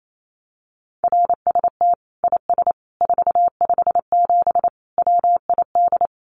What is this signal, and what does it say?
Morse code, heard as a keyed tone, about 28 wpm, 700 Hz. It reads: RST IS 457 WID
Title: Common QSO Phrases 28wpm